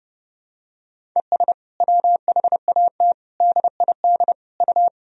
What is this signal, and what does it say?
Morse code, heard as a keyed tone, about 32 wpm, 700 Hz. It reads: ES WHAT DID U
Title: Sentences from Top 300 Words and Abbreviations 30wpm